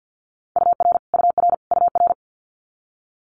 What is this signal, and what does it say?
Morse code, heard as a keyed tone, about 50 wpm, 700 Hz. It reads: UR UR UR
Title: Multi-letter Phonemes Repeated 3x 50wpm